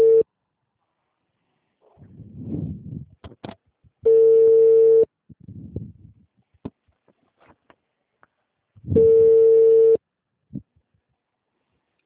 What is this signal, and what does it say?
Ringback tone (China, 450 Hz).